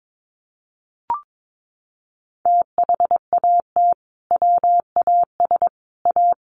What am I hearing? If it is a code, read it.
Morse code, heard as a keyed tone, about 22 wpm, 700 Hz. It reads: THAT WAS A
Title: Sentences from Top 900 Words 22wpm